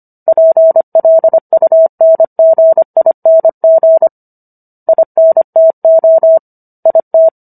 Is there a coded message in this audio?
Yes: Morse code from 0.3 to 7.3 s, heard as a keyed tone, about 25 wpm, 650 Hz. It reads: PLUNGING INTO IT